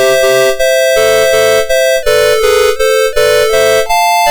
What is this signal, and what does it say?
There is no code.